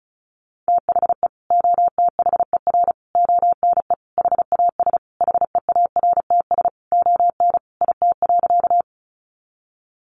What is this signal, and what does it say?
Morse code, heard as a keyed tone, about 34 wpm, 700 Hz. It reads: THE OTHER ONE HAS HEARTS ON IT.
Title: Additional Easy Phrases 35wpm